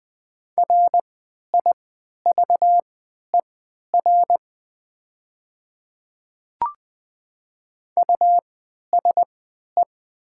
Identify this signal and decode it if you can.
Morse code, heard as a keyed tone, about 20 wpm, spaced as for 11 wpm, 700 Hz. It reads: RIVER USE
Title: 3x Character Spacing - Top 300 Words - 20wpm